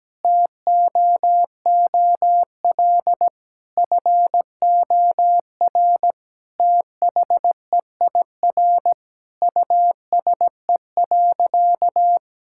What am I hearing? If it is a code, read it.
Morse code, heard as a keyed tone, about 17 wpm, 700 Hz. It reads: TOOL FOR THEIR USE.
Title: Sentences from Top 1000 Words - Review Part 6 - 17wpm